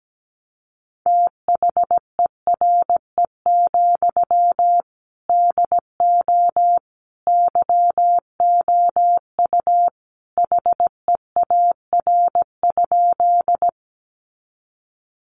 Morse code, heard as a keyed tone, about 17 wpm, 700 Hz. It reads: THERE, DO YOU HEAR?
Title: Sentences from Top 300 Words 17wpm